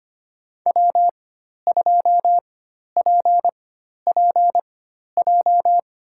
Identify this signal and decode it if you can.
Morse code, heard as a keyed tone, about 25 wpm, spaced as for 12 wpm, 700 Hz. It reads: W2PPJ
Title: USA Call Signs - 5 Characters - 4x Character Spacing - 25wpm